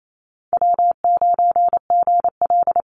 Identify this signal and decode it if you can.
Morse code, heard as a keyed tone, about 28 wpm, 700 Hz. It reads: W9GL
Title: Easy Call Signs - 28wpm